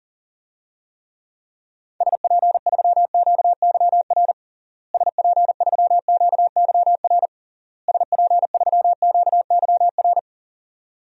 Morse code, heard as a keyed tone, about 40 wpm, 700 Hz. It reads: SP3QYR SP3QYR SP3QYR
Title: DXCC-100 Call Signs - 6 Characters - Repeated 3x - 40wpm